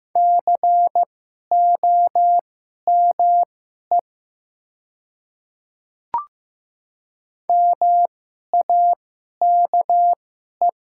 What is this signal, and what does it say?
Morse code, heard as a keyed tone, about 15 wpm, spaced as for 11 wpm, 700 Hz. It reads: COME MAKE
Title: 2x Character Spacing - Top 200 Words - 15wpm